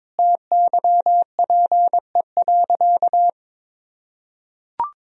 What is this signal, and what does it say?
Morse code, heard as a keyed tone, about 21 wpm, 700 Hz. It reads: TYPE.
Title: Sentences from Top 1000 Words - Review Part 3 - 22wpm